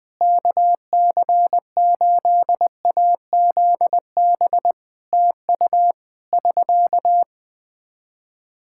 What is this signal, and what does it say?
Morse code, heard as a keyed tone, about 20 wpm, 700 Hz. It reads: KC8AZB TU <SK>
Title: Practice QSO - Exchange 4 - Ending - 20wpm